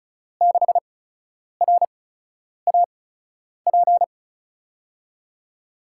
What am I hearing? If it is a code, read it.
Morse code, heard as a keyed tone, about 36 wpm, spaced as for 10 wpm, 700 Hz. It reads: 6RAP
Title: USA Call Signs - 5 Characters - 8x Character Spacing - 35wpm